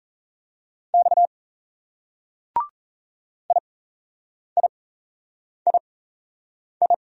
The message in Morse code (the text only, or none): X IIII